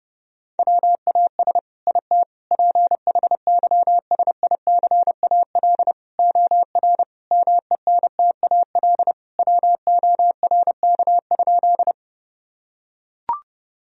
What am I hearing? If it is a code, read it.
Morse code, heard as a keyed tone, about 30 wpm, 700 Hz. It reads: WAS IT PHYSICAL OR MENTAL WORK?
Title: Sentences from Top 1000 Words - Review Part 4 - 30wpm